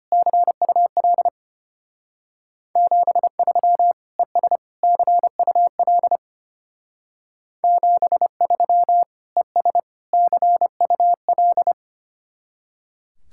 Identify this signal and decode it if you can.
Morse code, heard as a keyed tone, about 32 wpm, 700 Hz. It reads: CUL 73 ES CUL 73 ES CUL